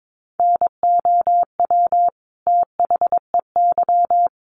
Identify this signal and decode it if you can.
Morse code, heard as a keyed tone, about 22 wpm, 700 Hz. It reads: NOW THEY
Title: Sentences from Top 700 Words 22wpm